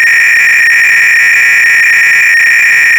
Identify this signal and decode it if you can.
300-baud modem data.
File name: ACR-Audio-Cassette-Programs / CharlesSwiger-Donations / KING cload-k